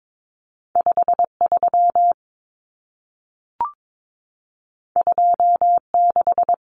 Morse code, heard as a keyed tone, about 22 wpm, 700 Hz. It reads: 53 26
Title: Numbers - 2-Digit 22wpm